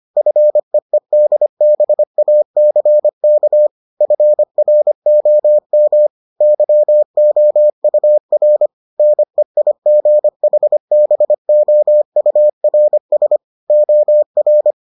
Morse code, heard as a keyed tone, about 25 wpm, 600 Hz. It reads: FEEDBACK FROM YOUR NEIGHBOURS OR